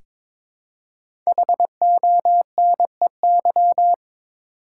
Morse code, heard as a keyed tone, about 22 wpm, 700 Hz. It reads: HONEY
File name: 5 Letter Words - First 3 Letters Given - 22wpm